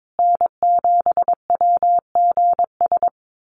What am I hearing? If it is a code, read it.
Morse code, heard as a keyed tone, about 22 wpm, 700 Hz. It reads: N7WGS
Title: USA Call Signs Repeated 3x 22wpm